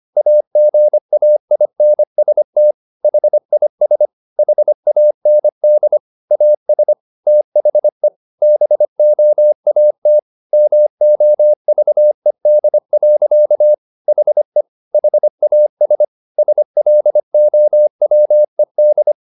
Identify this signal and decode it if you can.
Morse code, heard as a keyed tone, about 25 wpm, 600 Hz. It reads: AGAINST HIS HAND AS THE BOAT MOVED. HE HAS SLOWED
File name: the old man and the sea part 1 25wpm